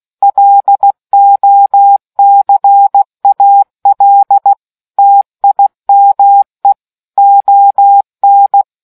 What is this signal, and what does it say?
Morse code, heard as a keyed tone, about 16 wpm, 800 Hz. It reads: LOCAL TIME ON